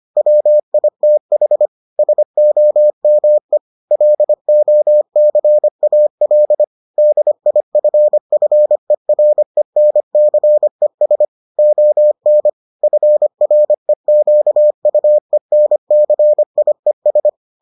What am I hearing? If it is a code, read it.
Morse code, heard as a keyed tone, about 25 wpm, 600 Hz. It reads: WITH SOME LOCAL DIFFERENCES ON FREQUENCIES